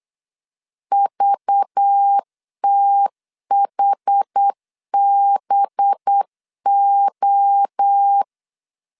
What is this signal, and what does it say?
Morse code, heard as a keyed tone, about 8 wpm, 800 Hz. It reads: VTHBO
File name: LEZIONI / 6 PKX0 / 6 PKX 8wpm 3041